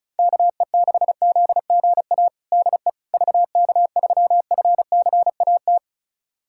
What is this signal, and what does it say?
Morse code, heard as a keyed tone, about 36 wpm, 700 Hz. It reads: KE6ZGA DE VK3FCAT
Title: USA Callsign DE Callsign 35wpm